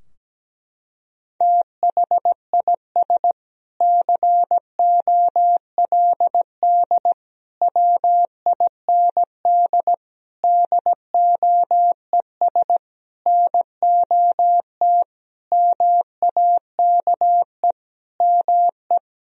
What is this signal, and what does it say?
Morse code, heard as a keyed tone, about 17 wpm, 700 Hz. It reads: THIS COLD WIND DOES NOT MAKE ME
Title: Sentences from Top 1000 Words - Review Part 2 - 17wpm